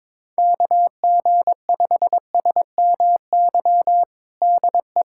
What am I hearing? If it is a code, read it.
Morse code, heard as a keyed tone, about 22 wpm, 700 Hz. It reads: KG5SMY DE
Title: USA Callsign DE Callsign - 6 Characters - 22wpm